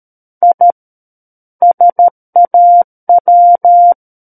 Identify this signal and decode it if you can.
Morse code, heard as a keyed tone, about 13 wpm, 700 Hz. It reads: I SAW
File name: Book One / 04-Chapter 4 - 13 WPM